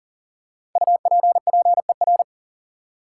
Morse code, heard as a keyed tone, about 40 wpm, 700 Hz. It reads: UPPER
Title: Five Letter Words - 40wpm